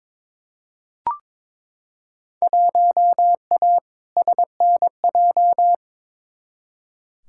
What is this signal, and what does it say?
Morse code, heard as a keyed tone, about 22 wpm, 700 Hz. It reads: 1A SNJ
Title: Field Day Exchanges - 22wpm